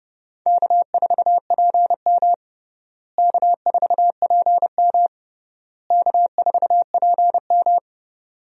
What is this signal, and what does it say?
Morse code, heard as a keyed tone, about 30 wpm, 700 Hz. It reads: K4PM K4PM K4PM
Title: USA Call Signs - 4 Characters - Repeated 3x - 30wpm